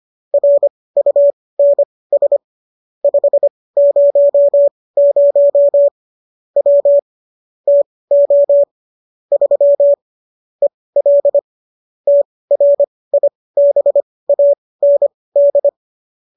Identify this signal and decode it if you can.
Morse code, heard as a keyed tone, about 25 wpm, spaced as for 18 wpm, 570 Hz. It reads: RUNS 500 W TO 3 EL TRIBAND